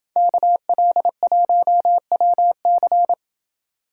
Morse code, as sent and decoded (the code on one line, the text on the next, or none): -.- .-.. .---- .-- -.-.
KL1WC